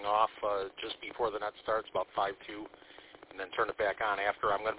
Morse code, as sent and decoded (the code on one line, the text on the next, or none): none